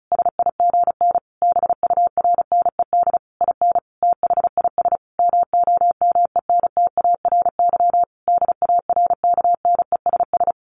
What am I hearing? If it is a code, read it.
Morse code, heard as a keyed tone, about 34 wpm, 700 Hz. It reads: SIGN BURNED IN THIS MOMENTARY DARKNESS
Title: A Sound Of Thunder 35wpm